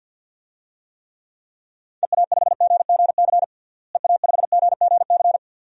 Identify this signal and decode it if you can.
Morse code, heard as a keyed tone, about 50 wpm, 700 Hz. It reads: EA5GGC EA5GGC
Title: DXCC-100 Call Signs - 6 Characters - Repeated 2x - 50wpm